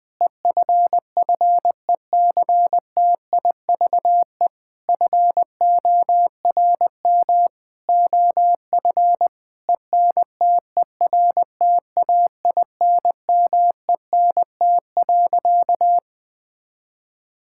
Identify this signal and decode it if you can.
Morse code, heard as a keyed tone, about 20 wpm, 700 Hz. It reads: EFFECTIVE FORM OF ENTERTAINMENT.
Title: Sentences from Top 1000 Words - Review Part 6 - 20wpm